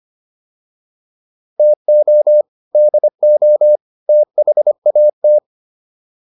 Morse code, heard as a keyed tone, about 25 wpm, 600 Hz. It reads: TO DO THAT